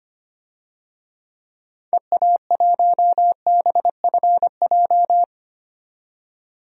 Morse code, heard as a keyed tone, about 25 wpm, 700 Hz. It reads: EA1BFJ
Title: DXCC-100 Call Signs - 6 Characters - 25wpm